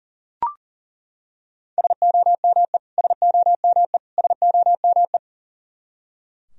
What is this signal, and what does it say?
Morse code, heard as a keyed tone, about 40 wpm, 700 Hz. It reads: SOME SOME SOME
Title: Top 100 Words Repeated 40wpm